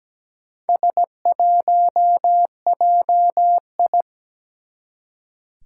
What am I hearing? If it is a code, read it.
Morse code, heard as a keyed tone, about 17 wpm, 700 Hz. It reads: S1JI
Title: Four Challenging Letter-Number Combinations 17wpm